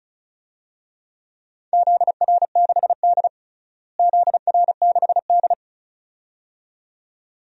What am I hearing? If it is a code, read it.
Morse code, heard as a keyed tone, about 36 wpm, 700 Hz. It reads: ZR6D ZR6D
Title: DXCC-100 Call Signs - 4 Characters - Repeated 2x - 35wpm